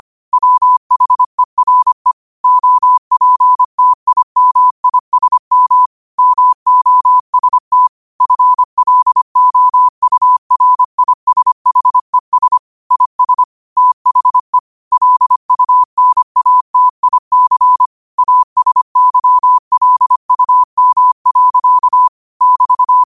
Morse code, heard as a keyed tone, about 25 wpm, 1000 Hz. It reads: WHERE OPTIMISM MOST FLOURISHES IS THE LUNATIC ASYLUM. <BT>